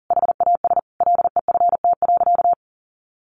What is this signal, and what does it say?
Morse code, heard as a keyed tone, about 40 wpm, 700 Hz. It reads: HAS LEFT.
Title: Sentences from Top 1500 Words 40wpm